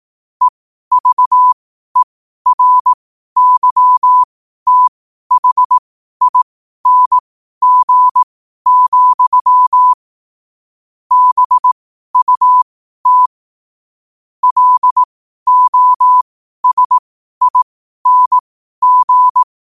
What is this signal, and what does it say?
Morse code, heard as a keyed tone, about 18 wpm, spaced as for 12 wpm, 1000 Hz. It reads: EVERYTHING, BUT LOSING
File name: qotdfiles / 12wpm